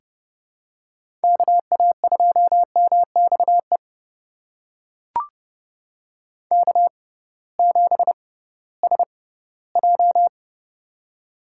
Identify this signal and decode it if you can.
Morse code, heard as a keyed tone, about 30 wpm, spaced as for 18 wpm, 700 Hz. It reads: KA2MXE K7SJ
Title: USA Call Signs - 6x Character Spacing - 30wpm